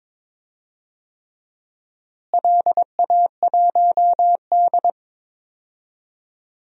Morse code, heard as a keyed tone, about 22 wpm, 700 Hz. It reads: LA1D
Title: DXCC-100 Call Signs - 4 Characters - Repeated 3x - 22wpm